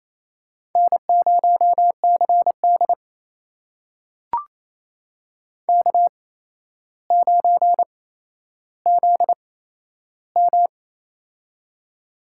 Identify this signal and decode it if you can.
Morse code, heard as a keyed tone, about 28 wpm, spaced as for 12 wpm, 700 Hz. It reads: N0CD K9ZM
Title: USA Call Signs - 4 Characters - 8x Character Spacing - 28wpm